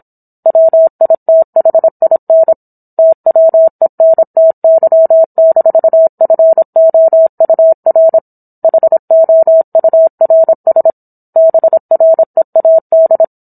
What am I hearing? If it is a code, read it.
Morse code, heard as a keyed tone, about 27 wpm, 650 Hz. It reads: WITHIN TWENTY-FOUR HOURS BREAD